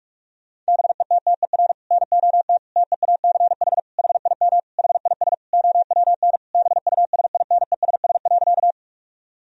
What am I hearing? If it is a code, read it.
Morse code, heard as a keyed tone, about 46 wpm, 700 Hz. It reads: BETTER NOT TEACH HIM HIS OWN BUSINESS.
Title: Sentences from Top 1000 Words - Review Part 2 - 45wpm